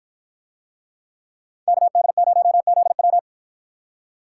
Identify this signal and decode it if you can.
Morse code, heard as a keyed tone, about 53 wpm, 700 Hz. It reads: KD0ZW